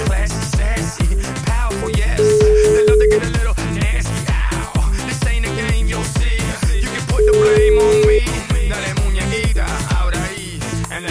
Ringback tone (Europe, 425 Hz), with noise.